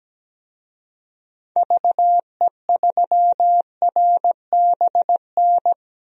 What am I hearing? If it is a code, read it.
Morse code, heard as a keyed tone, about 17 wpm, 700 Hz. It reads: VE3RBN
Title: DXCC-100 Call Signs - 6 Characters - 17wpm